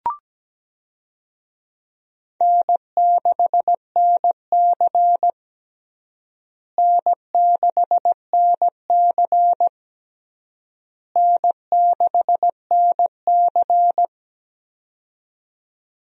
Morse code, heard as a keyed tone, about 17 wpm, 700 Hz. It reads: N6NC N6NC N6NC
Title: USA Call Signs - 4 Characters - Repeated 3x - 17wpm